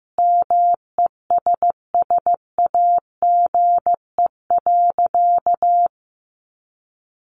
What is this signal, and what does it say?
Morse code, heard as a keyed tone, about 14 wpm, 700 Hz. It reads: MESSAGE.
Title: Sentences from Top 1100 Words 15wpm